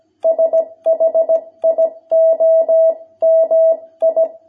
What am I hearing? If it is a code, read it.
Morse code, heard as a keyed tone, about 17 wpm, 660 Hz. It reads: SHIOMI
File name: jh8cbh / 2021cw